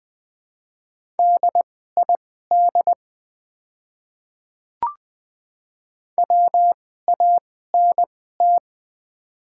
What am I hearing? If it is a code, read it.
Morse code, heard as a keyed tone, about 20 wpm, spaced as for 14 wpm, 700 Hz. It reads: DID WANT
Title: 2x Character Spacing - Top 200 Words - 20wpm